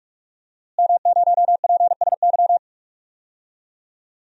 Morse code, heard as a keyed tone, about 46 wpm, 700 Hz. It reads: M0PSY
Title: DXCC-100 Call Signs - 5 Characters - Repeated 2x - 45wpm